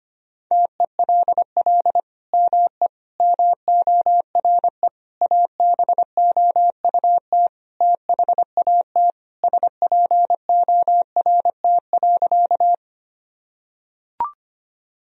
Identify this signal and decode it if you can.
Morse code, heard as a keyed tone, about 25 wpm, 700 Hz. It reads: TELL ME MORE ABOUT THAT SPORT.
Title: Sentences from Top 1000 Words - Review Part 4 - 25wpm